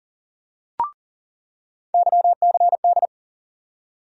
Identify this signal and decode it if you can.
Morse code, heard as a keyed tone, about 39 wpm, 700 Hz. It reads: YCD
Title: USA Call Sign Postfixes - 3 Letters 40wpm